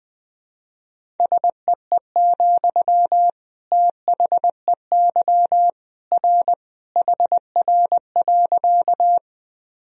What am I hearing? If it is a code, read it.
Morse code, heard as a keyed tone, about 20 wpm, 700 Hz. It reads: SEE, THEY R HR.
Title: Sentences from Top 200 Words and Abbreviations 20wpm